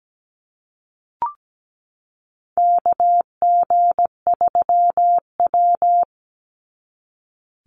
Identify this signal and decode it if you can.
Morse code, heard as a keyed tone, about 17 wpm, 700 Hz. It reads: KG3W
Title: NA SOTA Chaser Callsigns 17wpm